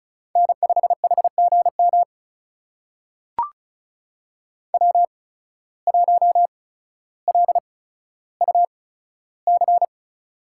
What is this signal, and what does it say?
Morse code, heard as a keyed tone, about 36 wpm, spaced as for 15 wpm, 700 Hz. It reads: N5HGM W1LUC
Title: USA Call Signs - 5 Characters - 8x Character Spacing - 35wpm